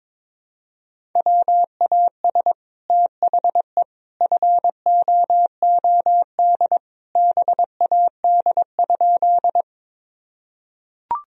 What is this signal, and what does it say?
Morse code, heard as a keyed tone, about 22 wpm, 700 Hz. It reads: WAS THE FOOD BAD?
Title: Sentences from Top 1000 Words - Review Part 1 - 22wpm